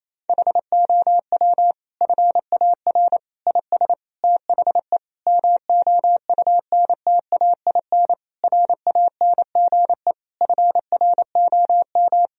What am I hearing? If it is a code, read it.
Morse code, heard as a keyed tone, about 28 wpm, 700 Hz. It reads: HOW FAR IS THE MOUNTAIN RANGE FROM
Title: Sentences from Top 1000 Words - Review Part 4 - 28wpm